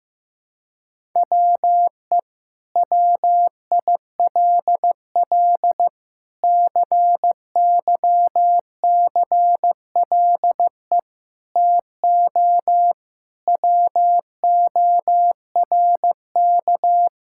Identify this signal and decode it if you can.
Morse code, heard as a keyed tone, about 15 wpm, 700 Hz. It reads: WE WILL CYCLE TO WORK
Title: Sentences from Top 1000 Words - Review Part 6 - 15wpm